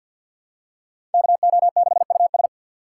Morse code, heard as a keyed tone, about 45 wpm, 700 Hz. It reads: KO6US
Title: USA Call Signs - 6x Character Spacing - 50wpm